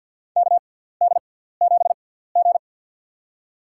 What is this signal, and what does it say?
Morse code, heard as a keyed tone, about 53 wpm, spaced as for 17 wpm, 700 Hz. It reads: KD7G
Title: USA Call Signs - 4 Characters - 6x Character Spacing - 50wpm